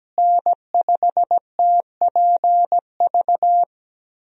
Morse code, heard as a keyed tone, about 17 wpm, 700 Hz. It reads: N5TPV
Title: USA Call Signs - 5 Characters - Repeated 2x - 17wpm